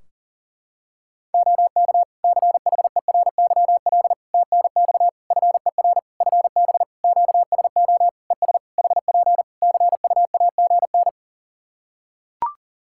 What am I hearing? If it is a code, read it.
Morse code, heard as a keyed tone, about 40 wpm, 700 Hz. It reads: OK CHERYL TNX FER FB QSO ES HP CUAGN
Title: Practice QSO - Exchange 4 - Ending - 40wpm